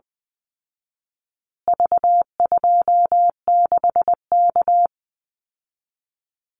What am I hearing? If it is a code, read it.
Morse code, heard as a keyed tone, about 20 wpm, 700 Hz. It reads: V26K
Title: Call Signs Repeated 2x - 20wpm